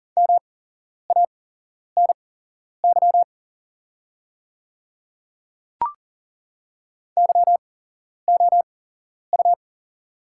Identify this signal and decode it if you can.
Morse code, heard as a keyed tone, about 40 wpm, spaced as for 11 wpm, 700 Hz. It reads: MANY YOU
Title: 8x Character Spacing - Top 300 Words - 40wpm